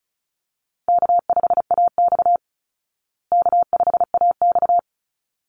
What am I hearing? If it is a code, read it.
Morse code, heard as a keyed tone, about 34 wpm, 700 Hz. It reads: K5AX K5AX
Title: Easy Call Signs Repeated 2x - 35wpm